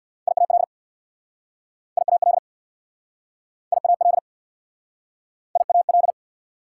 Morse code, heard as a keyed tone, about 69 wpm, 690 Hz. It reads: IAL IAL IAL IAL